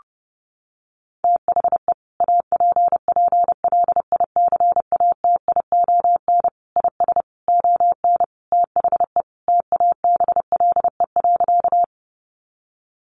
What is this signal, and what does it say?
Morse code, heard as a keyed tone, about 30 wpm, 700 Hz. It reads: THE APPLICATION IS ON THE TABLE.
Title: Sentences from Top 1200 Words 30wpm